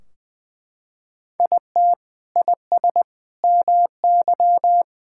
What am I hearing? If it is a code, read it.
Morse code, heard as a keyed tone, about 20 wpm, 700 Hz. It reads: IT IS MY
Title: Sentences from Top 1600 Words 20wpm